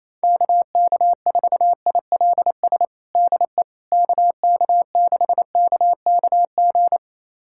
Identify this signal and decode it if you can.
Morse code, heard as a keyed tone, about 27 wpm, 700 Hz. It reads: KK4ILS DE KK6KKG
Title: USA Callsign DE Callsign - 6 Characters - 28wpm